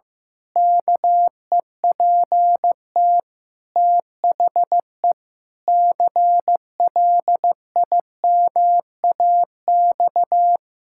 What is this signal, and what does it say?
Morse code, heard as a keyed tone, about 15 wpm, 700 Hz. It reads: KEPT THE CLIMAX
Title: Pangram Challenge v2 - 15wpm